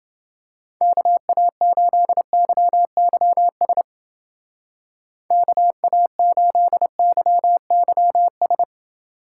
Morse code, heard as a keyed tone, about 29 wpm, 700 Hz. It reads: KA8YYS KA8YYS